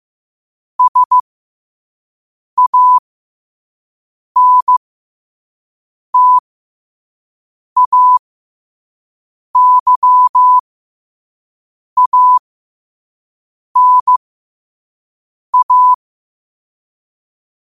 Morse code, heard as a keyed tone, about 15 wpm, spaced as for 5 wpm, 1000 Hz. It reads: SANTAYANA
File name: qotdfiles / 5wpm